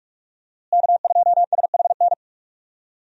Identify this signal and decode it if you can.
Morse code, heard as a keyed tone, about 46 wpm, 700 Hz. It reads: K2SHN
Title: USA Call Signs - 5 Characters - Repeated 2x - 45wpm